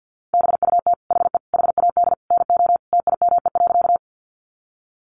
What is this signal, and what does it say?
Morse code, heard as a keyed tone, about 50 wpm, 700 Hz. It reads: BUT HE HAD NO TIME.
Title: Sentences from Top 100 Words 50wpm